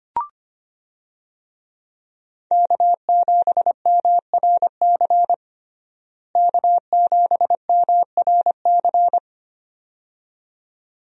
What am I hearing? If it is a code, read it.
Morse code, heard as a keyed tone, about 25 wpm, 700 Hz. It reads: K7MRC K7MRC
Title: USA Call Signs - 5 Characters - Repeated 2x - 25wpm